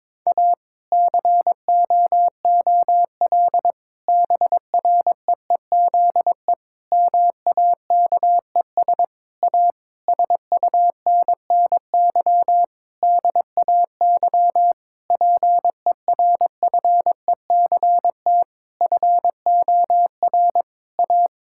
Morse code, heard as a keyed tone, about 22 wpm, 700 Hz. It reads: A COOL BREEZE MAKES A SUNNY DAY PERFECT FOR A
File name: Difficult Sentences with Hints - 22wpm